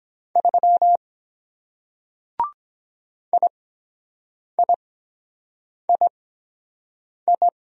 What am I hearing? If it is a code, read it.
Morse code, heard as a keyed tone, about 23 wpm, spaced as for 7 wpm, 700 Hz. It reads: 3 IIII